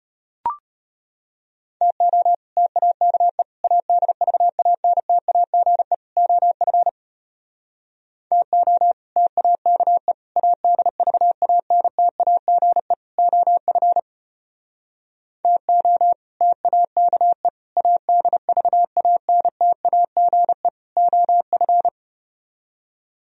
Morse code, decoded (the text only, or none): TO TAKE ADVANTAGE OF TO TAKE ADVANTAGE OF TO TAKE ADVANTAGE OF